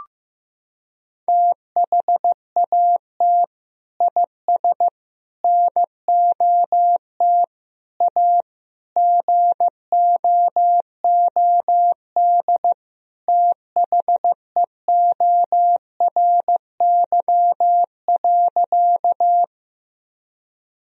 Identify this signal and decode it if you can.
Morse code, heard as a keyed tone, about 15 wpm, 700 Hz. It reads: THAT IS NOT A GOOD THEORY.
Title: Sentences from Top 1000 Words - Review Part 3 - 15wpm